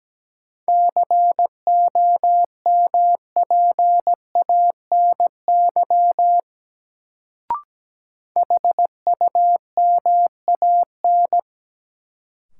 Morse code, heard as a keyed tone, about 17 wpm, 700 Hz. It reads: COMPANY HUMAN
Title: Sentences from Top 600 Words - New Word Drill - 17wpm